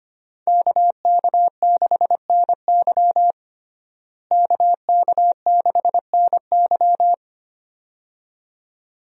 Morse code, heard as a keyed tone, about 25 wpm, 700 Hz. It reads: KK6NY KK6NY
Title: USA Call Signs - 5 Characters - Repeated 2x - 25wpm